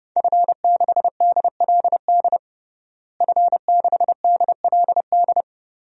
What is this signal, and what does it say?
Morse code, heard as a keyed tone, about 30 wpm, 700 Hz. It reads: F6DLD F6DLD
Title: DXCC-100 Call Signs - 5 Characters - Repeated 3x - 30wpm